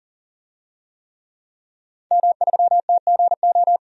Morse code, heard as a keyed tone, about 40 wpm, 700 Hz. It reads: M3TGO
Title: DXCC-100 Call Signs - 5 Characters - 40wpm